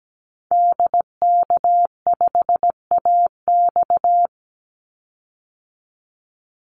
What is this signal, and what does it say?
Morse code, heard as a keyed tone, about 17 wpm, 700 Hz. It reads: DK5AX
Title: Call Signs Repeated 3x - 17wpm